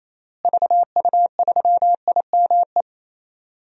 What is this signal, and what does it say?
Morse code, heard as a keyed tone, about 28 wpm, 700 Hz. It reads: VU3IME
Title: DXCC-100 Call Signs - 6 Characters - Repeated 3x - 28wpm